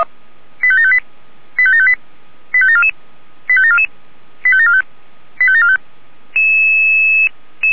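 Five-tone selective call (ZVEI), heard as quick runs of five tones, tones of about 70 ms. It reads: 85758 85758 85744 85744 85745 85745
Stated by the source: Five-tone ZVEI selective call, code 85726, received by SDR.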